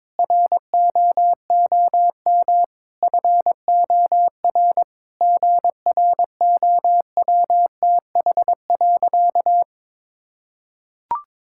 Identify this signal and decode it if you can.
Morse code, heard as a keyed tone, about 21 wpm, 700 Hz. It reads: ROOM FOR GROWTH.
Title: Sentences from Top 1000 Words - Review Part 5 - 22wpm